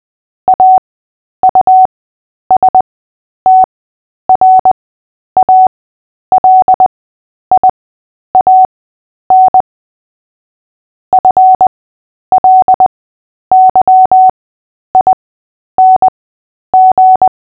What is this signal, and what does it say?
Morse code, heard as a keyed tone, about 20 wpm, spaced as for 10 wpm, 750 Hz. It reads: AUSTRALIAN FLYING
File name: Short-Story-7-10wpm